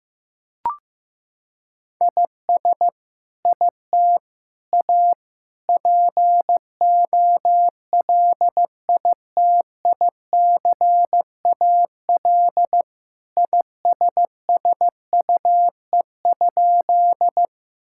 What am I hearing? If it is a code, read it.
Morse code, heard as a keyed tone, about 15 wpm, 700 Hz. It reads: IS IT A POLITICAL ISSUE?
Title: Sentences from Top 1000 Words - Review Part 6 - 15wpm